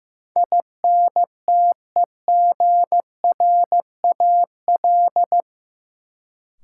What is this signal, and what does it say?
Morse code, heard as a keyed tone, about 15 wpm, 700 Hz. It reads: INTEGRAL
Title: Sets of 5 Words - New Word Drill - 15wpm